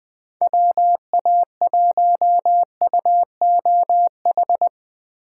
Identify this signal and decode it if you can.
Morse code, heard as a keyed tone, about 22 wpm, 700 Hz. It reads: WA1UOH
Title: Speed Racing - USA 2x3 Call Signs 20wpm